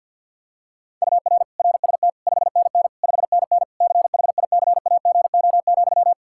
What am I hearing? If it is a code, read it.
Morse code, heard as a keyed tone, about 49 wpm, 700 Hz. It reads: UR RST 5NN 5NN CHICAGO,
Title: K1USN SST QSO Practice 50wpm